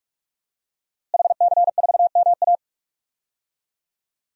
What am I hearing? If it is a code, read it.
Morse code, heard as a keyed tone, about 46 wpm, 700 Hz. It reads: HC4MA
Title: DXCC-100 Call Signs - 5 Characters - Repeated 2x - 45wpm